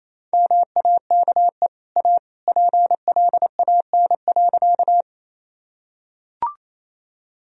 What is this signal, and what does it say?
Morse code, heard as a keyed tone, about 28 wpm, 700 Hz. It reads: MAKE A PLAN.
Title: Sentences from Top 1000 Words - Review Part 2 - 28wpm